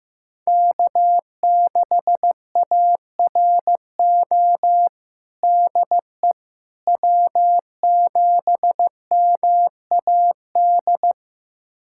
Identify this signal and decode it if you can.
Morse code, heard as a keyed tone, about 15 wpm, 700 Hz. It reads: K6ARO DE W7MAD
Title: USA Callsign DE Callsign - 5 Characters - 15wpm